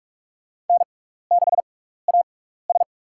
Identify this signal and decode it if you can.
Morse code, heard as a keyed tone, about 47 wpm, spaced as for 16 wpm, 700 Hz. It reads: N6AS
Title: USA Call Signs - 4 Characters - 6x Character Spacing - 45wpm